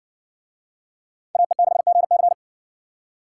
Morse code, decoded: AE6GC

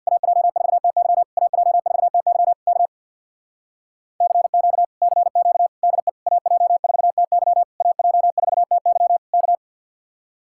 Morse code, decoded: AJ4TY AJ4TY K CQ CQ DE AJ4TY AJ4TY K

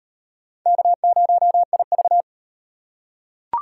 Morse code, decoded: K0IV

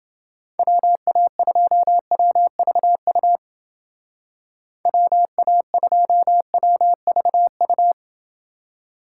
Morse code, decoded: WA2WVU WA2WVU

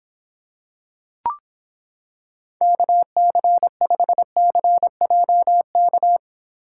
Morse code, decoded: KC5CJK

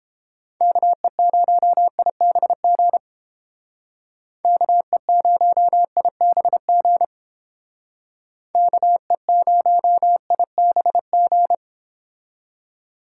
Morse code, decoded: KE0IBG KE0IBG KE0IBG

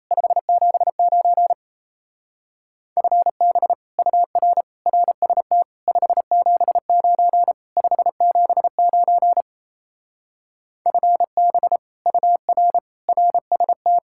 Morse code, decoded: 579 FB UR RST 579 579 FB UR RST

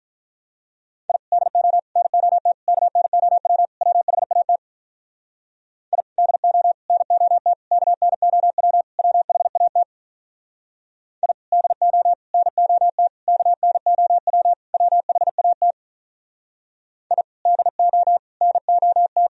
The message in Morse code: I DO NOT KNOW WHAT I DO NOT KNOW WHAT I DO NOT KNOW WHAT I DO NOT

